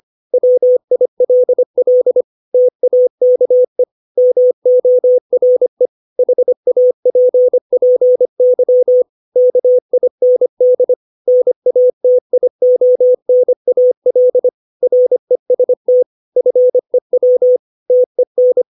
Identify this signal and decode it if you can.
Morse code, heard as a keyed tone, about 25 wpm, 500 Hz. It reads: WILL TAKE MORE HAPPY KIND NATIONAL REST FEW TEN
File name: 632 words / 632 Words 25WPM